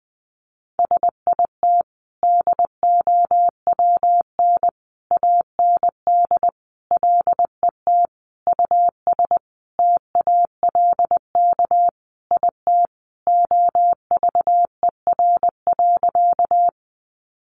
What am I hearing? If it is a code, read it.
Morse code, heard as a keyed tone, about 20 wpm, 700 Hz. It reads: SIT DOWN AND LET US TALK IT OVER.
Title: Sentences from Top 500 Words 20wpm